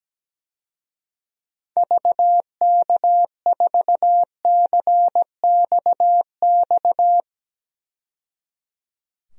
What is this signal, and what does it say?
Morse code, heard as a keyed tone, about 17 wpm, 700 Hz. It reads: VK4CXX
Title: DXCC-100 Call Signs - 6 Characters - 17wpm